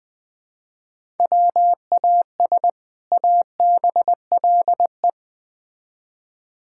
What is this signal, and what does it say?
Morse code, heard as a keyed tone, about 21 wpm, 700 Hz. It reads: WAS ABLE
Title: Build A Phrase - 3-Word Phrases - 20wpm